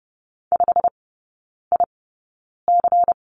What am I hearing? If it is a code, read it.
Morse code, heard as a keyed tone, about 30 wpm, spaced as for 9 wpm, 700 Hz. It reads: 5IC
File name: Lesson 16e - Letter c - Three Characters 30wpm